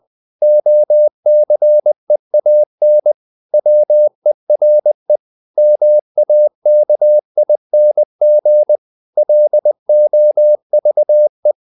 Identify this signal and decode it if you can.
Morse code, heard as a keyed tone, about 20 wpm, 600 Hz. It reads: OCEAN WERE MAKING LOVE
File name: the old man and the sea part 2 20wpm